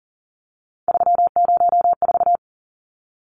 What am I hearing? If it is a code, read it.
Morse code, heard as a keyed tone, about 40 wpm, 700 Hz. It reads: 304
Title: Numbers - 3-Digit 40wpm